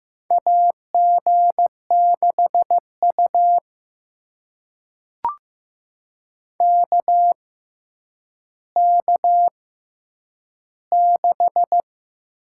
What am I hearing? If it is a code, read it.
Morse code, heard as a keyed tone, about 15 wpm, spaced as for 9 wpm, 700 Hz. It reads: AG6U KK6
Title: USA Call Signs - 4 Characters - 6x Character Spacing - 15wpm